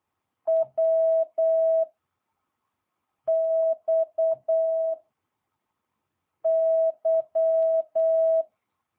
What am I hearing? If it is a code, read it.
Morse code, heard as a keyed tone, about 8 wpm, 650 Hz. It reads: W X Y